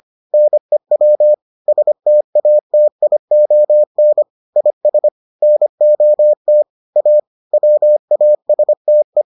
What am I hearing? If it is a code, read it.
Morse code, heard as a keyed tone, about 25 wpm, 600 Hz. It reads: NEW STATION IS NOT A WASTE